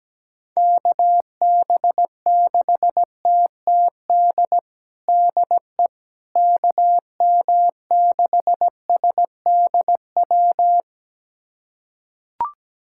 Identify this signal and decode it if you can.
Morse code, heard as a keyed tone, about 17 wpm, 700 Hz. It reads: KB6TTD DE KM6SDW
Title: USA Callsign DE Callsign - 6 Characters - 17wpm